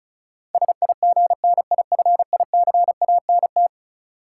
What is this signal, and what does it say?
Morse code, heard as a keyed tone, about 36 wpm, 700 Hz. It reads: SIGNIFICANT
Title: Sentences from Top 1000 Words - Review Part 6 - New Word Drill - 35wpm